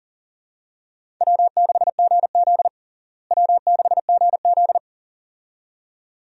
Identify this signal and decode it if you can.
Morse code, heard as a keyed tone, about 37 wpm, 700 Hz. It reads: W6GZ W6GZ
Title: USA Call Signs - 4 Characters - Repeated 2x - 40wpm